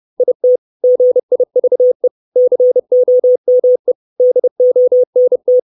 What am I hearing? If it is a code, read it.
Morse code, heard as a keyed tone, about 30 wpm, 500 Hz. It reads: IT GIVE COME DONT
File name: 632 words / 632 Words 30WPM